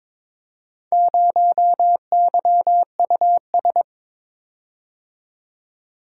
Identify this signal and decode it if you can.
Morse code, heard as a keyed tone, about 22 wpm, 700 Hz. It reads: 0YUS
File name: practice / 22wpm / Four Letter-Number Combinations - 22wpm